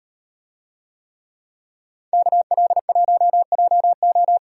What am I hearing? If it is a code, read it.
Morse code, heard as a keyed tone, about 35 wpm, 700 Hz. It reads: KL1JO